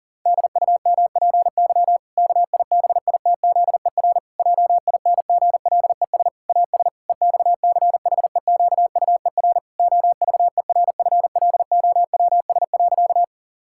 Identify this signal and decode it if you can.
Morse code, heard as a keyed tone, about 40 wpm, 700 Hz. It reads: DUMPY KIBITZER JINGLES AS EXCHEQUER OVERFLOWS.
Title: Pangram Challenge v2 - 40wpm